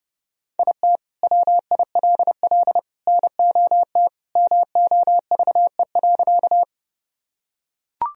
Morse code, heard as a keyed tone, about 30 wpm, 700 Hz. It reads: IT WILL NOT MOVE.
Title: Sentences from Top 1000 Words - Review Part 1 - 30wpm